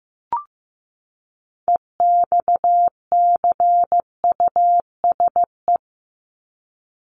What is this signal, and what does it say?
Morse code, heard as a keyed tone, about 15 wpm, 700 Hz. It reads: EXCUSE
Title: Top 501-750 Words 15wpm